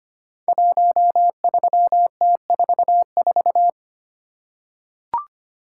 Morse code, heard as a keyed tone, about 25 wpm, 700 Hz. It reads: 13T44
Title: Code Groups - 5 Letter:Numbers - 25wpm